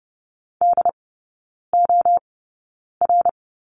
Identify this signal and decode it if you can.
Morse code, heard as a keyed tone, about 30 wpm, spaced as for 9 wpm, 700 Hz. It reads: DOR
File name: course / 30:10wpm / 087 Lesson 15e - Number 5 - Three Characters 30wpm